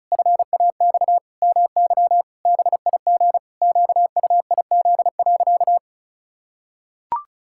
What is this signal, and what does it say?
Morse code, heard as a keyed tone, about 34 wpm, 700 Hz. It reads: FAX MY BIG QUIZ.
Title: Pangram Challenge v2 - 35wpm